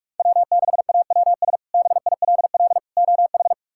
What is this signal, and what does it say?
Morse code, heard as a keyed tone, about 42 wpm, 700 Hz. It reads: W6AWS BILL OH
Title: K1USN SST QSO Practice - 45wpm